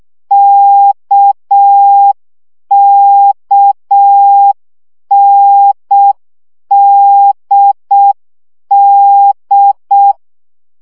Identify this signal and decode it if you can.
Morse code, heard as a keyed tone, about 6 wpm, 800 Hz. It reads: KKNDD
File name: CW gyakorlas n d k 5 baud